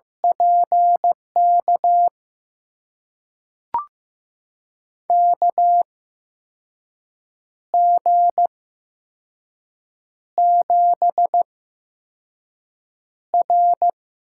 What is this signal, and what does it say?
Morse code, heard as a keyed tone, about 15 wpm, spaced as for 5 wpm, 700 Hz. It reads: PK KG7R
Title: USA Call Signs - 6 Characters - 8x Character Spacing - 15wpm